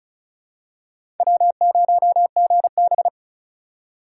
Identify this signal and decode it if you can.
Morse code, heard as a keyed tone, about 36 wpm, 690 Hz. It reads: W0GB
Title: USA Call Signs - 4x Character Spacing - 35wpm